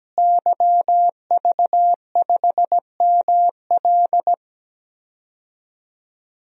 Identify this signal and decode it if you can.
Morse code, heard as a keyed tone, about 17 wpm, 700 Hz. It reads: YV5ML
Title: DXCC-100 Call Signs - 5 Characters - Repeated 3x - 17wpm